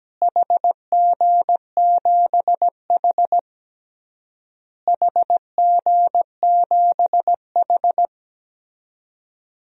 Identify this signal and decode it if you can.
Morse code, heard as a keyed tone, about 17 wpm, 700 Hz. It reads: HG7H HG7H
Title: DXCC-100 Call Signs - 4 Characters - Repeated 3x - 17wpm